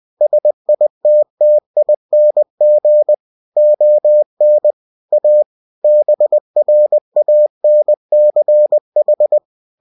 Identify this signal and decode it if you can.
Morse code, heard as a keyed tone, about 20 wpm, 600 Hz. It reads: SITTING ON A BRANCH